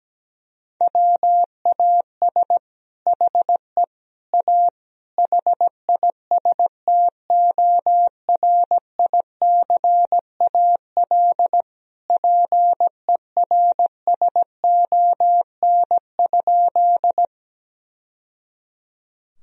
Morse code, heard as a keyed tone, about 17 wpm, 700 Hz. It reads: WAS HE A HISTORICAL PERSON?
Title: Sentences from Top 1000 Words - Review Part 4 - 17wpm